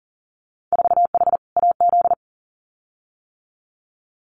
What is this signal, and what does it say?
Morse code, heard as a keyed tone, about 41 wpm, 700 Hz. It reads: 4H AZ
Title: Winter Field Day Exchanges Encore 40wpm